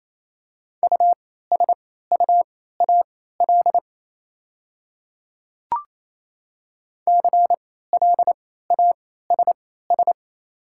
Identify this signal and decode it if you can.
Morse code, heard as a keyed tone, about 27 wpm, spaced as for 16 wpm, 690 Hz. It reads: USUAL CLASS
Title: 3x Character Spacing - Top 500 Words - 28wpm